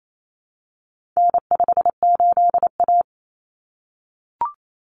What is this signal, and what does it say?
Morse code, heard as a keyed tone, about 28 wpm, 700 Hz. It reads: N58A
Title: Common Cut Numbers - 4-Digit 28wpm